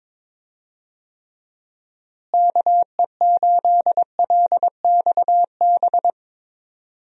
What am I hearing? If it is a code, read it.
Morse code, heard as a keyed tone, about 22 wpm, 700 Hz. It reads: KE8LXB
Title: USA Call Signs - 6 Characters - Repeated 2x - 22wpm